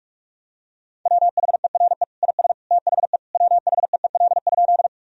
Morse code, heard as a keyed tone, about 41 wpm, 700 Hz. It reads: WHERE IS THE WHEEL?
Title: Sentences from Top 1000 Words - Review Part 3 - 45wpm